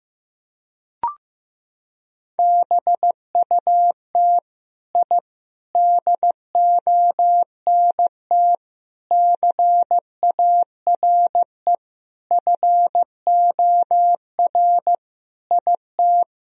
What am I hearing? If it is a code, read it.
Morse code, heard as a keyed tone, about 15 wpm, 700 Hz. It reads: BUT I DONT CARE FOR IT
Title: Sentences from Top 1000 Words - Review Part 1 - 15wpm